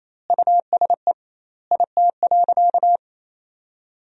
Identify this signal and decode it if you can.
Morse code, heard as a keyed tone, about 28 wpm, 700 Hz. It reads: USE IT.
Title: 2x Word-Spacing - Sentences from Top 500 Words - 28wpm